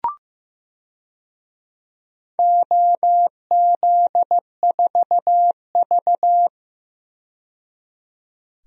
Morse code, heard as a keyed tone, about 16 wpm, 700 Hz. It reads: OZ4V
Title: DXCC-100 Call Signs - 4 Characters - 15wpm